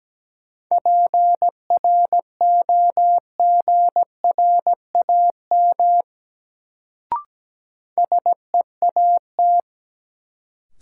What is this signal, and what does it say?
Morse code, heard as a keyed tone, about 17 wpm, 700 Hz. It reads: PROGRAM SEAT
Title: Sentences from Top 1500 Words - New Word Drill Alt - 17wpm